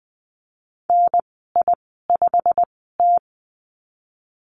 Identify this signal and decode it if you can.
Morse code, heard as a keyed tone, about 20 wpm, spaced as for 14 wpm, 700 Hz. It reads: NI5T
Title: 088 Lesson 15g - Number 5 - Call Signs 20wpm